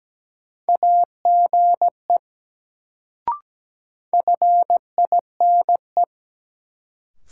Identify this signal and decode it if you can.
Morse code, heard as a keyed tone, about 17 wpm, 700 Hz. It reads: AGE FINE
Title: Sentences from Top 500 Words - New Word Drill - 17wpm